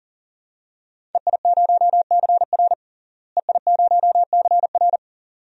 Morse code, heard as a keyed tone, about 40 wpm, 700 Hz. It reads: EI0CR EI0CR
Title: DXCC-100 Call Signs - 5 Characters - Repeated 2x - 40wpm